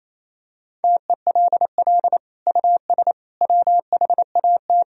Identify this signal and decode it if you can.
Morse code, heard as a keyed tone, about 28 wpm, 700 Hz. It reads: TELL US WHAT
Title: Sentences from Top 1000 Words - Review Part 4 - 28wpm